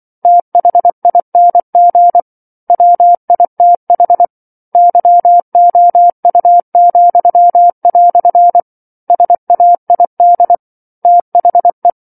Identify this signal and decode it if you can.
Morse code, heard as a keyed tone, about 24 wpm, 700 Hz. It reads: THING WITH YOU," SAID THE